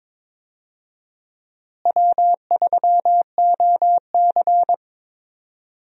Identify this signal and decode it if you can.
Morse code, heard as a keyed tone, about 22 wpm, 700 Hz. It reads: W3OC
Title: USA Call Signs - 4 Characters - Repeated 2x - 22wpm